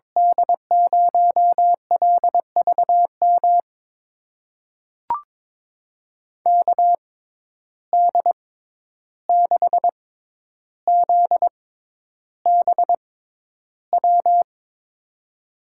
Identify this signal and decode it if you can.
Morse code, heard as a keyed tone, about 22 wpm, spaced as for 11 wpm, 700 Hz. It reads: D0LVM KD6ZBW
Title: USA Call Signs - 6 Characters - 6x Character Spacing - 22wpm